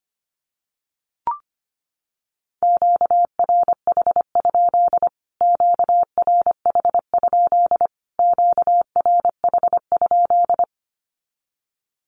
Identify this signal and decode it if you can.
Morse code, heard as a keyed tone, about 25 wpm, 700 Hz. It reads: QRH? QRH? QRH?
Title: Q-Codes Repeated 3x 25wpm